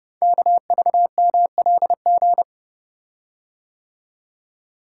Morse code, heard as a keyed tone, about 30 wpm, 700 Hz. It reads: KVMLG
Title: Code Groups - 5 Letters - 30wpm